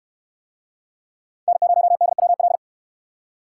Morse code, heard as a keyed tone, about 68 wpm, 700 Hz. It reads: N0DWL